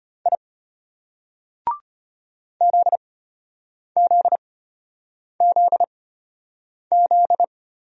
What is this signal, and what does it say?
Morse code, heard as a keyed tone, about 34 wpm, spaced as for 8 wpm, 700 Hz. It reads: I ZZZZ